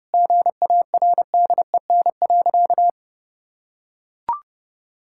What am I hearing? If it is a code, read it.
Morse code, heard as a keyed tone, about 30 wpm, 700 Hz. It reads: GARDEN.
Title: Sentences from Top 1000 Words - Review Part 5 - 30wpm